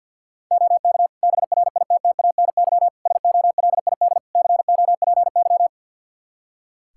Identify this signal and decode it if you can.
Morse code, heard as a keyed tone, about 51 wpm, 700 Hz. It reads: OK BRITTANY SOLID COPY
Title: K1USN SST QSO Practice 50wpm